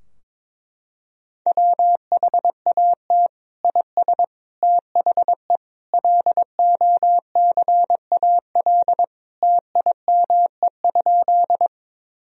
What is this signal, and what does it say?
Morse code, heard as a keyed tone, about 22 wpm, 700 Hz. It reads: WHAT IS THE LOCAL TIME?
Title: Sentences from Top 1000 Words - Review Part 3 - 22wpm